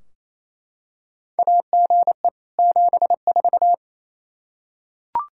Morse code, heard as a keyed tone, about 28 wpm, 700 Hz. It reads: AGE 74
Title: Practice QSO - Exchange 3 - More Info - 28wpm